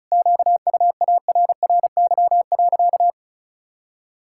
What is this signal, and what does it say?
Morse code, heard as a keyed tone, about 36 wpm, 700 Hz. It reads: QUARRY.
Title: Pangram Challenge v2 - 35wpm